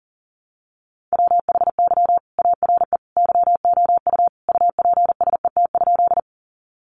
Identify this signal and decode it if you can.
Morse code, heard as a keyed tone, about 40 wpm, 700 Hz. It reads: WHY ARE YOU UPSET?
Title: Sentences from Top 800 Words 40wpm